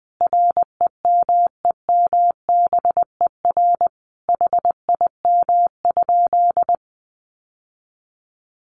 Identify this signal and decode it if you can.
Morse code, heard as a keyed tone, about 20 wpm, 700 Hz. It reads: REMEMBER HIM?
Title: Sentences from Top 600 Words 20wpm